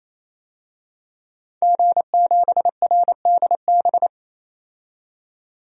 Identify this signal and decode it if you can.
Morse code, heard as a keyed tone, about 28 wpm, 700 Hz. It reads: G7RDB
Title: DXCC-100 Call Signs - 5 Characters - 28wpm